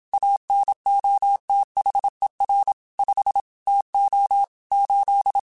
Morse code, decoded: ANOTHER 5 TO 8

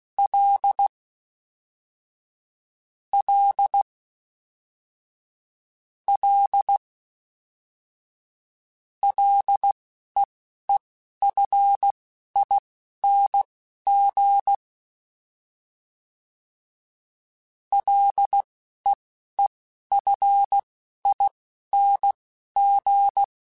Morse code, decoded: L L L LEEFING LEEFING

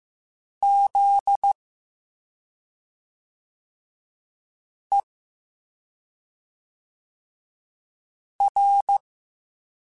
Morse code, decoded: ZER

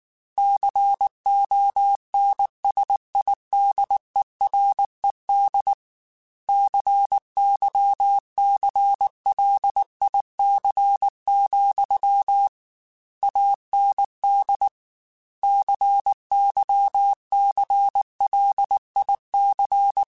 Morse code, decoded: CONSIDERED CYCLIC, AND CYCLIC

